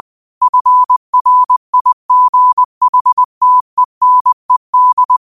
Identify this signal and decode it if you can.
Morse code, heard as a keyed tone, about 20 wpm, 1000 Hz. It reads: FRIGHTENED
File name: qotdfiles / 20wpm